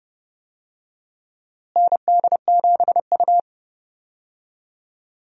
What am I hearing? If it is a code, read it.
Morse code, heard as a keyed tone, about 32 wpm, 700 Hz. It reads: ND7U
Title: USA Call Signs - 4 Characters - Repeated 3x - 30wpm